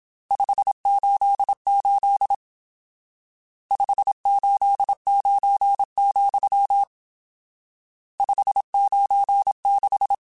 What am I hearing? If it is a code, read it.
Morse code, heard as a keyed tone, about 27 wpm, 800 Hz. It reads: 588 589, 596